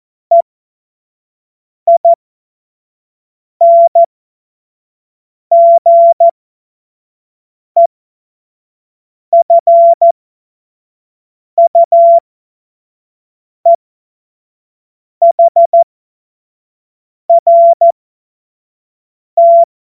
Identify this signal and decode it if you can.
Morse code, heard as a keyed tone, about 14 wpm, spaced as for 5 wpm, 680 Hz. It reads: EINGEFUEHRT
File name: Witz-29-Juli-2025-25-F70